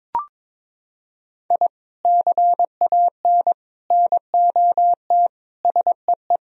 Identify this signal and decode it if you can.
Morse code, heard as a keyed tone, about 22 wpm, 700 Hz. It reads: I CAN NOT SEE
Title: Sentences from Top 1000 Words - Review Part 4 - 22wpm